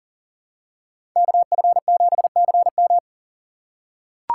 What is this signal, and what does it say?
Morse code, heard as a keyed tone, about 40 wpm, 700 Hz. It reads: KF7CM
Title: USA Call Signs - 5 Characters - 4x Character Spacing - 40wpm